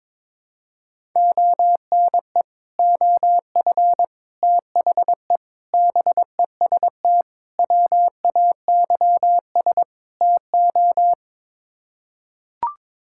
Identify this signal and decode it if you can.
Morse code, heard as a keyed tone, about 22 wpm, 700 Hz. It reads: ONE OF THE BEST WAYS TO
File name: practice / 22wpm / Sets of 6 Words - 22wpm